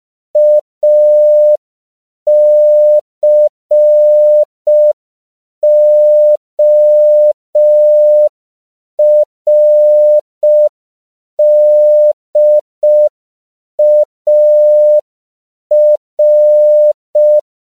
Morse code, heard as a keyed tone, about 5 wpm, 600 Hz. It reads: ACORDAR
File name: CW / 5PPM